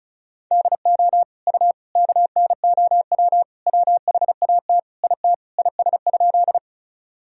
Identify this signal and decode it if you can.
Morse code, heard as a keyed tone, about 34 wpm, 700 Hz. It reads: DO U KNOW WHAT IT IS?
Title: Sentences from Top 100 Words and Abbreviations 35wpm